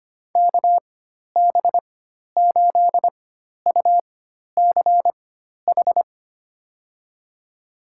Morse code, heard as a keyed tone, about 25 wpm, spaced as for 12 wpm, 700 Hz. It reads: KB8UCH
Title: USA Call Signs - 4x Character Spacing - 25wpm